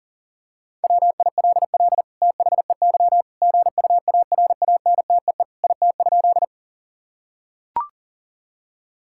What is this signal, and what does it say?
Morse code, heard as a keyed tone, about 40 wpm, 700 Hz. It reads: WILL THEY GUARANTEE IT?
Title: Sentences from Top 1600 Words 40wpm